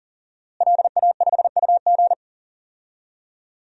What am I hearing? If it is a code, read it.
Morse code, heard as a keyed tone, about 40 wpm, 700 Hz. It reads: LA5UG
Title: DXCC-100 Call Signs - 5 Characters - Repeated 2x - 40wpm